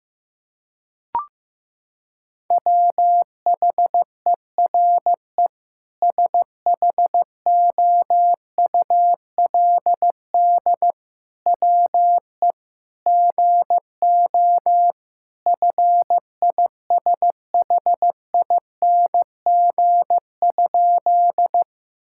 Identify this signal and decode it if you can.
Morse code, heard as a keyed tone, about 15 wpm, 700 Hz. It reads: WHERE SHOULD WE GO FISHING?
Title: Sentences from Top 1000 Words - Review Part 5 - 15wpm